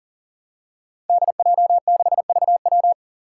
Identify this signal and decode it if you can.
Morse code, heard as a keyed tone, about 41 wpm, 700 Hz. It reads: DJ6VW
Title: Five Challenging Letter-Number Combinations 40wpm